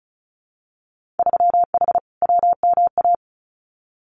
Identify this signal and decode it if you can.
Morse code, heard as a keyed tone, about 36 wpm, 700 Hz. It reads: 3H WMA
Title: Winter Field Day Exchanges Encore 35wpm